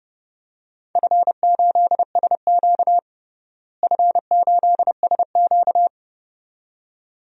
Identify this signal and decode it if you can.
Morse code, heard as a keyed tone, about 29 wpm, 700 Hz. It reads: F8SQ F8SQ
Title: DXCC-100 Call Signs - 4 Characters - Repeated 2x - 30wpm